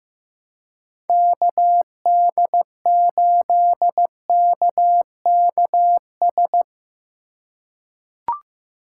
Morse code, heard as a keyed tone, about 15 wpm, 700 Hz. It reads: KD8KKS
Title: USA Call Signs - 6 Characters - 4x Character Spacing - 15wpm